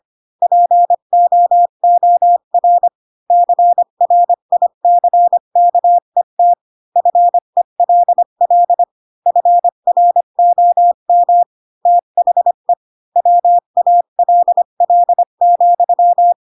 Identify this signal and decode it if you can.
Morse code, heard as a keyed tone, about 25 wpm, 700 Hz. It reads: POOR CRICKET FELL FROM THE WALL,
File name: Pinocchio / Pinocchio ch4 20 25 ik1hge